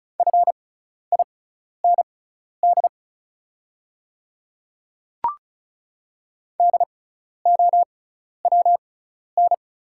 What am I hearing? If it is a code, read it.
Morse code, heard as a keyed tone, about 36 wpm, spaced as for 12 wpm, 700 Hz. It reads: FIND DOWN
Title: 6x Character Spacing - Top 100 Words - 35wpm